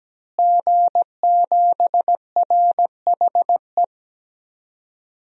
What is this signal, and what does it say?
Morse code, heard as a keyed tone, about 17 wpm, 700 Hz. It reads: G7RHE
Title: DXCC-100 Call Signs - 5 Characters - Repeated 3x - 17wpm